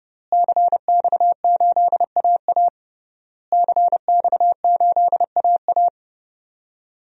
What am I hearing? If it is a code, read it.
Morse code, heard as a keyed tone, about 30 wpm, 700 Hz. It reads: CX8AA CX8AA
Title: DXCC-100 Call Signs - 5 Characters - Repeated 3x - 30wpm